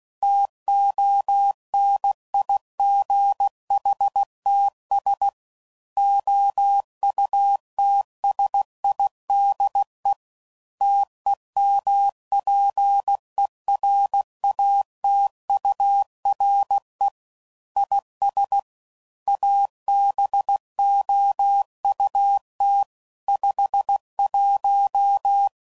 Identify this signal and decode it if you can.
Morse code, heard as a keyed tone, about 16 wpm, 800 Hz. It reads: TONIGHTS OUTSIDE TEMPERATURE IS ABOUT 51
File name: CW Code Groups @15WPM / Weather Sets / Weather 4 21